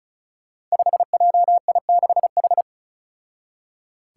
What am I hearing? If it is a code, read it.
Morse code, heard as a keyed tone, about 36 wpm, 700 Hz. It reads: 5JI6H
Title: Five Challenging Letter-Number Combinations 35wpm